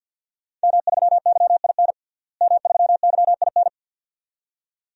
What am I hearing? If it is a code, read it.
Morse code, heard as a keyed tone, about 50 wpm, 700 Hz. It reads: M3YIN M3YIN
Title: DXCC-100 Call Signs - 5 Characters - Repeated 2x - 50wpm